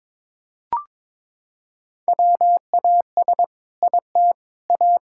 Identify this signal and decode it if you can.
Morse code, heard as a keyed tone, about 22 wpm, 700 Hz. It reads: WAS IT A
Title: Sentences from Top 1000 Words - Review Part 6 - 22wpm